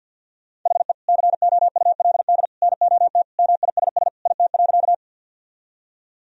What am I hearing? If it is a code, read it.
Morse code, heard as a keyed tone, about 50 wpm, 700 Hz. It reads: HE COULD NOT MISS IT.
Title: Sentences from Top 1000 Words - Review Part 3 - 50wpm